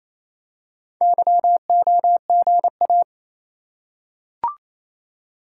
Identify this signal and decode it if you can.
Morse code, heard as a keyed tone, about 27 wpm, 700 Hz. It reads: YOGA
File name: Easy Words - Speed-Racing with Hints - 22wpm